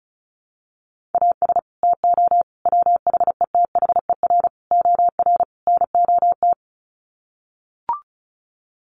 Morse code, heard as a keyed tone, about 34 wpm, 700 Hz. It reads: AS TO WHETHER OR NOT
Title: Sets of 5 Words Encore 35wpm